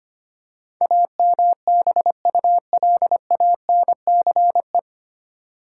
Morse code, heard as a keyed tone, about 25 wpm, 700 Hz. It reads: AMBULANCE
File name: practice / 25wpm / 9 Letter Words - First 3 Letters Given - 25wpm